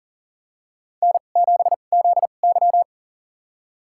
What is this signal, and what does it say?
Morse code, heard as a keyed tone, about 40 wpm, spaced as for 29 wpm, 700 Hz. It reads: N7ZY
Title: USA Call Signs - 2x Character Spacing - 40wpm